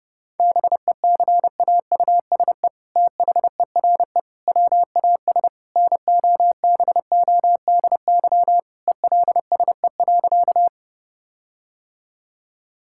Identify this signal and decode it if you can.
Morse code, heard as a keyed tone, about 30 wpm, 700 Hz. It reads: BECAUSE THERE WAS NOBODY ELSE.
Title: Sentences from Top 1600 Words 30wpm